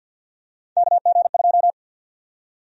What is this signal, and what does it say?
Morse code, heard as a keyed tone, about 51 wpm, 700 Hz. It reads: KG2
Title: USA Call Sign Prefixes - 50wpm